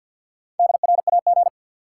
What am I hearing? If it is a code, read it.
Morse code, heard as a keyed tone, about 50 wpm, 700 Hz. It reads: DRAG
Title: Sentences from Top 1500 Words - New Word Drill Alt - 50wpm